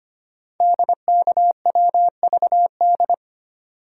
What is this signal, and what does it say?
Morse code, heard as a keyed tone, about 25 wpm, 700 Hz. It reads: DKWVD
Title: Code Groups - 5 Letters - 25wpm